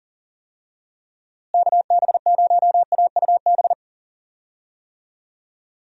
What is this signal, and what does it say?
Morse code, heard as a keyed tone, about 40 wpm, 700 Hz. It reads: KB0AUB
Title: USA Call Signs - 6 Characters - Repeated 3x - 40wpm